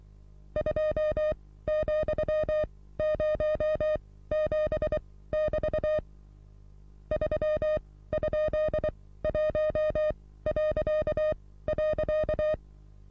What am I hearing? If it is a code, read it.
Morse code, heard as a keyed tone, about 23 wpm, spaced as for 16 wpm, 620 Hz. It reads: 2,07<BT> 3?1..
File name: repos / Morsekurs / DJ4UF Morsen27